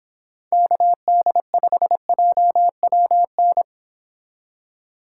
Morse code, heard as a keyed tone, about 25 wpm, 700 Hz. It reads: KD5JWN